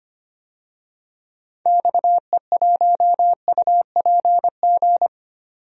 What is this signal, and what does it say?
Morse code, heard as a keyed tone, about 25 wpm, 700 Hz. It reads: XE1UPG
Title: DXCC-100 Call Signs - 6 Characters - 25wpm